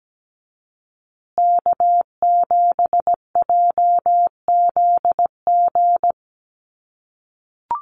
Morse code, heard as a keyed tone, about 17 wpm, 700 Hz. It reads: K7JZG
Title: USA Call Signs Repeated 3x 17wpm